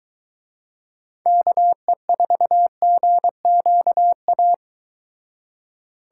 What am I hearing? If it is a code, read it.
Morse code, heard as a keyed tone, about 23 wpm, 700 Hz. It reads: KE4GQA